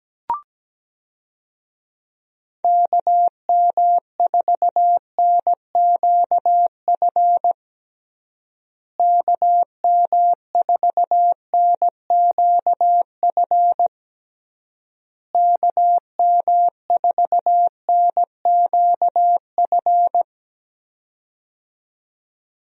Morse code, heard as a keyed tone, about 17 wpm, 700 Hz. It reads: KM4NQF KM4NQF KM4NQF
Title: USA Call Signs - 6 Characters - Repeated 3x - 17wpm